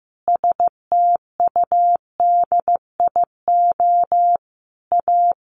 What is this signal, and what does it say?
Morse code, heard as a keyed tone, about 15 wpm, 700 Hz. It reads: STUDIO A
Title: Sentences from Top 1200 Words 15wpm